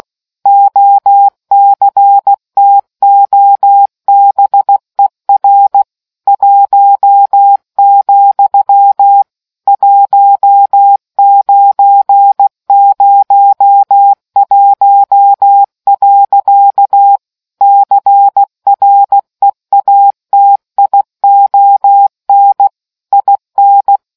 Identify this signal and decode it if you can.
Morse code, heard as a keyed tone, about 16 wpm, 800 Hz. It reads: OCTOBER 1, 1901. CREATION IN